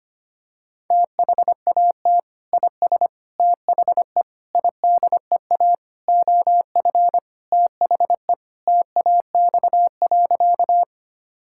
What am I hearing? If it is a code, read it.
Morse code, heard as a keyed tone, about 25 wpm, 700 Hz. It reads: THAT IS THE IDEA OF THE TAX.
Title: Sentences from Top 1000 Words - Review Part 5 - 25wpm